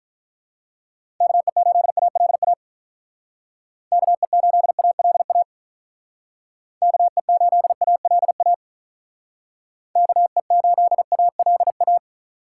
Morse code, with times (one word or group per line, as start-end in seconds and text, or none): 1.2-2.5 KE8ALA
3.9-5.4 KE8ALA
6.8-8.5 KE8ALA
10.0-12.0 KE8ALA